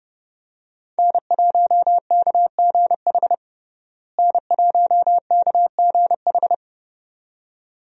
Morse code, heard as a keyed tone, about 30 wpm, 700 Hz. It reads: N1KGH N1KGH
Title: USA Call Signs - 5 Characters - Repeated 2x - 30wpm